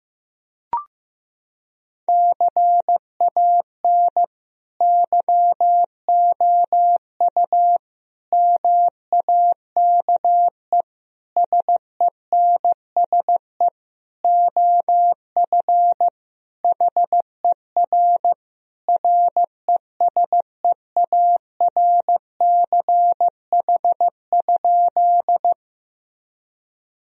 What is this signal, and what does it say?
Morse code, heard as a keyed tone, about 15 wpm, 700 Hz. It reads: CAN YOU MAKE SENSE OF HER RESEARCH?
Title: Sentences from Top 1000 Words - Review Part 4 - 15wpm